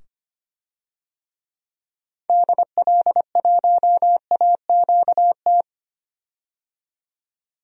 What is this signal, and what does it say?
Morse code, heard as a keyed tone, about 25 wpm, 700 Hz. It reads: DL1AQT
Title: DXCC-100 Call Signs - 6 Characters - 25wpm